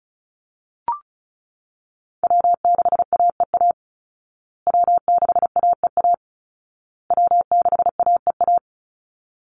Morse code, heard as a keyed tone, about 34 wpm, 700 Hz. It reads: W6AEA W6AEA W6AEA
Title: Call Signs Repeated 2x - 35wpm